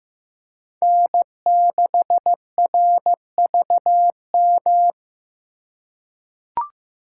Morse code, heard as a keyed tone, about 15 wpm, 700 Hz. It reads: N6RVM
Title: USA Call Signs - 5 Characters - 4x Character Spacing - 15wpm